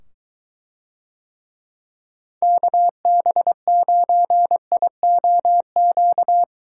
Morse code, heard as a keyed tone, about 23 wpm, 700 Hz. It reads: KB9IOQ